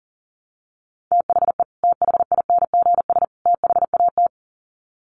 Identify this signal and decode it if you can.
Morse code, heard as a keyed tone, about 40 wpm, 700 Hz. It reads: THE THINGS THAT
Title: Sets of 3 Words 40wpm